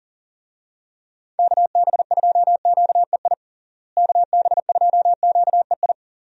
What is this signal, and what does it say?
Morse code, heard as a keyed tone, about 37 wpm, 700 Hz. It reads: KB2QEI KB2QEI
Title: USA Call Signs - 6 Characters - Repeated 2x - 40wpm